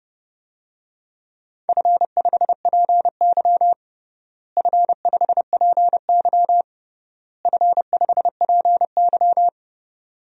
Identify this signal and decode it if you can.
Morse code, heard as a keyed tone, about 30 wpm, 700 Hz. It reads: F5PY F5PY F5PY
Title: DXCC-100 Call Signs - 4 Characters - Repeated 3x - 30wpm